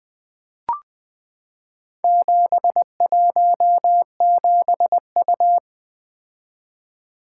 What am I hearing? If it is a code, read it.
Morse code, heard as a keyed tone, about 20 wpm, 700 Hz. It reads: 717U
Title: Four Letter-Number Combinations - 20wpm